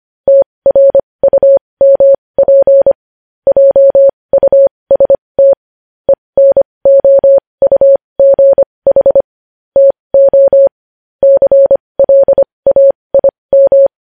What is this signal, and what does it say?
Morse code, heard as a keyed tone, about 25 wpm, spaced as for 20 wpm, 550 Hz. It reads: TRUMP JUST ENOUGH TO CLAIM
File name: podcast / News headlines-1x-25x20